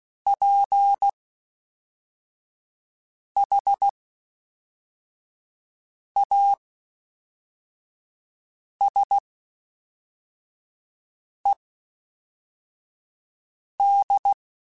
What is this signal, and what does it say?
Morse code, heard as a keyed tone, about 16 wpm, spaced as for 4 wpm, 800 Hz. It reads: PHASED